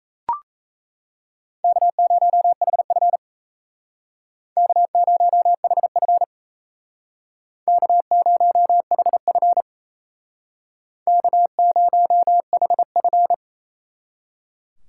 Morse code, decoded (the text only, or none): K0HF K0HF K0HF K0HF